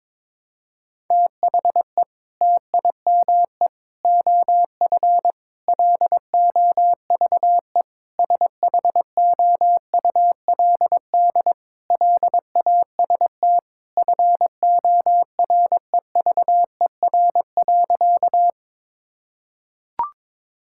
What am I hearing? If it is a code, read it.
Morse code, heard as a keyed tone, about 22 wpm, 700 Hz. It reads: THE TIME OF LOVE SHOULD LAST FOREVER.
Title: Sentences from Top 1600 Words 22wpm